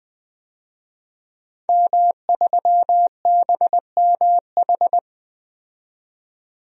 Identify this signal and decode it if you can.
Morse code, heard as a keyed tone, about 20 wpm, 700 Hz. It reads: M3BMH
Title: DXCC-100 Call Signs - 5 Characters - 20wpm